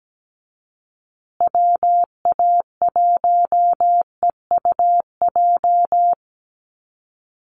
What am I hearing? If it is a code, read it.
Morse code, heard as a keyed tone, about 17 wpm, 700 Hz. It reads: WA1EUJ
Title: USA Call Signs Repeated 3x 17wpm